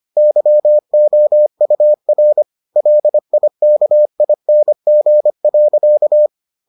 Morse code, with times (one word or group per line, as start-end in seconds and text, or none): 0.2-2.4 YOUR
2.8-6.3 LIKING.